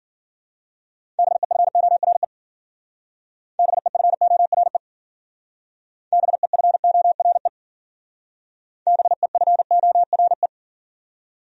Morse code, heard as a keyed tone, about 52 wpm, 700 Hz. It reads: BEFORE BEFORE BEFORE BEFORE